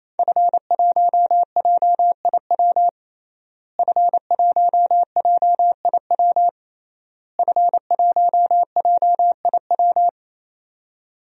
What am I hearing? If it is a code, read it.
Morse code, heard as a keyed tone, about 27 wpm, 700 Hz. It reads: F1JIW F1JIW F1JIW
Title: DXCC-100 Call Signs - 5 Characters - Repeated 3x - 28wpm